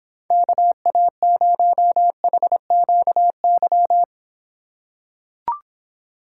Morse code, heard as a keyed tone, about 26 wpm, 700 Hz. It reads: KA0HQY